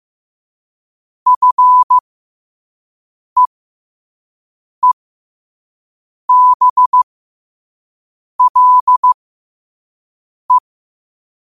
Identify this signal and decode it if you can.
Morse code, heard as a keyed tone, about 15 wpm, spaced as for 5 wpm, 1000 Hz. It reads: FEEBLE
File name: qotdfiles / 5wpm